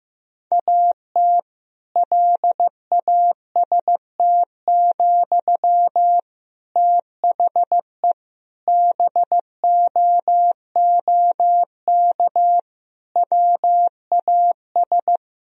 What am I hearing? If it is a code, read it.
Morse code, heard as a keyed tone, about 16 wpm, 700 Hz. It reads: AT LAST, THE BOOK WAS
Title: Sentences from Top 1000 Words - Review Part 3 - 15wpm